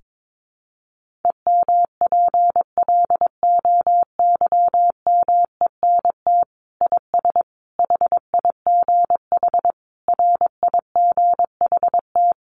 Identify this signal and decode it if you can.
Morse code, heard as a keyed tone, about 22 wpm, 700 Hz. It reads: EMPLOYMENT IS HIGH RIGHT
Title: Sentences from Top 1500 Words 22wpm